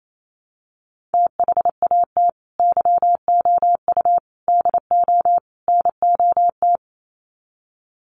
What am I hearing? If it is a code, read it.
Morse code, heard as a keyed tone, about 28 wpm, 700 Hz. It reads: THAT YOU DO NOT
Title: Sets of 4 Words 28wpm